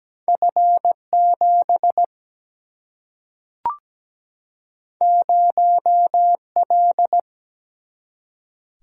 Morse code, heard as a keyed tone, about 17 wpm, 700 Hz. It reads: F7 0L
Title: Single-Letter Single-Number 17wpm